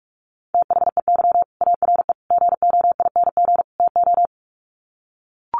Morse code, heard as a keyed tone, about 44 wpm, 700 Hz. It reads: THEY ARE GOING TO
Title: Sets of 4 Words 45wpm